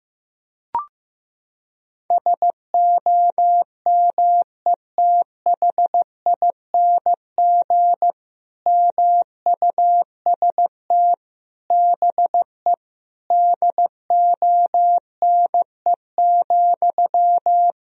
Morse code, heard as a keyed tone, about 15 wpm, 700 Hz. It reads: SOMETHING MUST BE DONE,
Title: Sentences from Top 1000 Words - Review Part 1 - 15wpm